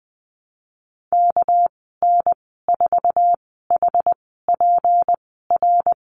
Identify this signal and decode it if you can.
Morse code, heard as a keyed tone, about 20 wpm, spaced as for 14 wpm, 700 Hz. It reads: KN4HPR
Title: 245 Lesson 39g - Letter z - Call Signs 20wpm